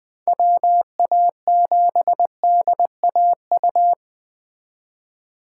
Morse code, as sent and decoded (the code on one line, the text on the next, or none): .-- .- --... -.. .- ..-
WA7DAU